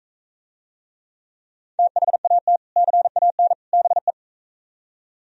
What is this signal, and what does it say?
Morse code, heard as a keyed tone, about 45 wpm, 700 Hz. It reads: THAT CAN BE